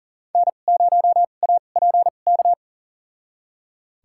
Morse code, heard as a keyed tone, about 40 wpm, spaced as for 29 wpm, 700 Hz. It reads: N0APK
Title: USA Call Signs - 5 Characters - 2x Character Spacing - 40wpm